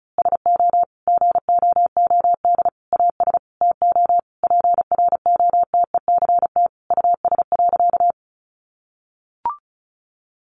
Morse code, heard as a keyed tone, about 34 wpm, 700 Hz. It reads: SO GOOD AS TO PROTECT US.
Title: Sentences from Top 900 Words 35wpm